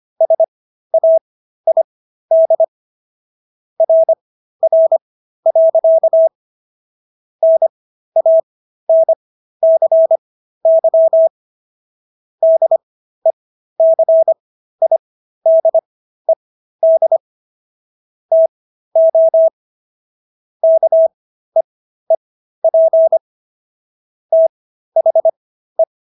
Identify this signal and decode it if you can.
Morse code, heard as a keyed tone, about 25 wpm, spaced as for 13 wpm, 650 Hz. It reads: SAID RR. NANCY DECIDED TO KEEP THE